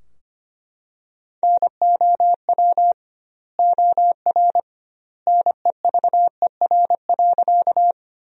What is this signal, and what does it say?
Morse code, heard as a keyed tone, about 25 wpm, 700 Hz. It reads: NOW OR NEVER.
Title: 2x Word-Spacing - Sentences from Top 300 Words - 25wpm